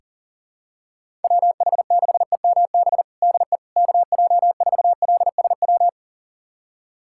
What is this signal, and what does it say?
Morse code, heard as a keyed tone, about 40 wpm, 700 Hz. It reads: WH6EMB DE KJ4LSW
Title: USA Callsign DE Callsign 40wpm